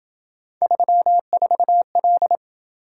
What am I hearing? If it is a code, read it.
Morse code, heard as a keyed tone, about 30 wpm, 700 Hz. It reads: 34L